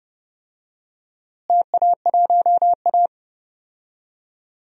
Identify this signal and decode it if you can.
Morse code, heard as a keyed tone, about 30 wpm, 700 Hz. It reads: TA1A
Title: DXCC-100 Call Signs - 4 Characters - Repeated 2x - 30wpm